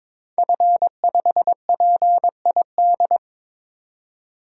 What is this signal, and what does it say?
Morse code, heard as a keyed tone, about 21 wpm, 700 Hz. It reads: F5PID
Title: DXCC-100 Call Signs - 5 Characters - 22wpm